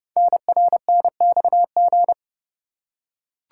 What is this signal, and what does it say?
Morse code, heard as a keyed tone, about 30 wpm, 700 Hz. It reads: NRNXG
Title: Code Groups - 5 Letters - 30wpm